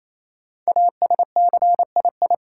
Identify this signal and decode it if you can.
Morse code, heard as a keyed tone, about 28 wpm, 700 Hz. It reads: ASCII
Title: Sets of 5 Words - New Word Drill - 28wpm